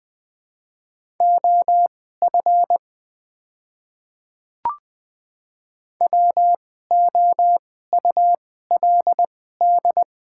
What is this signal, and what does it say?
Morse code, heard as a keyed tone, about 20 wpm, spaced as for 14 wpm, 700 Hz. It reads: OF WOULD
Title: 2x Character Spacing - Top 200 Words - 20wpm